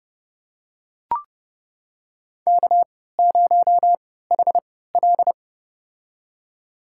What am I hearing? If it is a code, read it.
Morse code, heard as a keyed tone, about 30 wpm, spaced as for 17 wpm, 700 Hz. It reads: K0HL
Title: USA Call Signs - 4 Characters - 3x Character Spacing - 30wpm